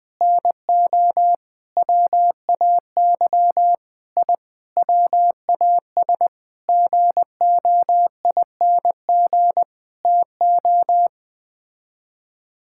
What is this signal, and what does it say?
Morse code, heard as a keyed tone, about 20 wpm, 700 Hz. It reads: NO WAY I WAS GOING TO
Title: Sets of 8 Words - 20wpm